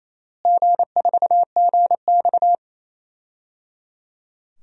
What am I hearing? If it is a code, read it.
Morse code, heard as a keyed tone, about 28 wpm, 700 Hz. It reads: G4GX
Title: DXCC-100 Call Signs - 4 Characters - Repeated 3x - 28wpm